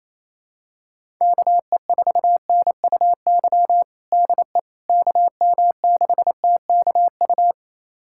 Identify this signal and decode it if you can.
Morse code, heard as a keyed tone, about 28 wpm, 700 Hz. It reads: KE4NUY DE KM6TKU
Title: USA Callsign DE Callsign 28wpm